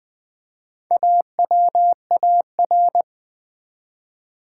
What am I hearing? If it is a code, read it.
Morse code, heard as a keyed tone, about 20 wpm, 700 Hz. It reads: AWAR
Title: Build A Word - 1 Letter At A Time - 3 to 5-Letter Words - 20wpm